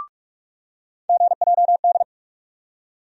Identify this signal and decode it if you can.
Morse code, heard as a keyed tone, about 42 wpm, 700 Hz. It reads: GJD
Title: USA Call Sign Postfixes - 3 Letters 45wpm